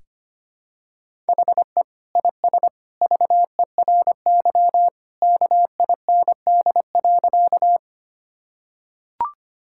Morse code, heard as a keyed tone, about 25 wpm, 700 Hz. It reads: HE IS VERY KIND.
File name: practice / 25wpm / Sentences with 2-Word Phrases - 25wpm